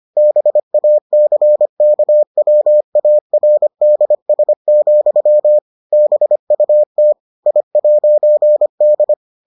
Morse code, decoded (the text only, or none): BACKWARDS, BUT I'D